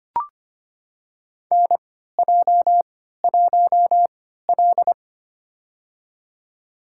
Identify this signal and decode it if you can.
Morse code, heard as a keyed tone, about 25 wpm, spaced as for 14 wpm, 700 Hz. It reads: NJ1L
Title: USA Call Signs - 4 Characters - 3x Character Spacing - 25wpm